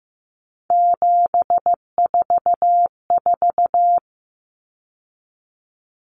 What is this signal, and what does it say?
Morse code, heard as a keyed tone, about 15 wpm, 700 Hz. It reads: 744
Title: Common Cut Numbers - 3-Digit 15wpm